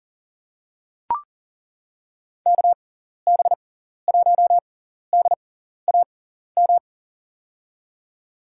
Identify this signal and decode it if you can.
Morse code, heard as a keyed tone, about 40 wpm, spaced as for 14 wpm, 700 Hz. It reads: KB1DAM
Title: USA Call Signs - 6 Characters - 6x Character Spacing - 40wpm